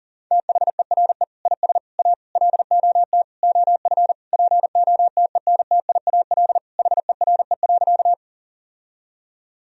Morse code, heard as a keyed tone, about 40 wpm, 700 Hz. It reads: THERE IS A LOT OF POTENTIAL HERE.
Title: Sentences from Top 1000 Words - Review Part 4 - 40wpm